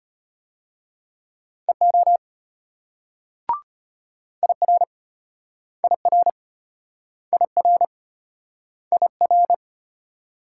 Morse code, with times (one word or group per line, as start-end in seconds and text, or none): 1.7-2.2 EO
4.4-4.8 IR
5.8-6.3 IR
7.3-7.9 IR
8.9-9.5 IR